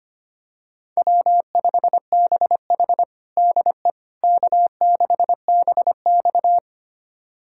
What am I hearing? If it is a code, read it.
Morse code, heard as a keyed tone, about 25 wpm, 700 Hz. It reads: W5BH DE K6BX
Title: USA Callsign DE Callsign - 4 Characters - 25wpm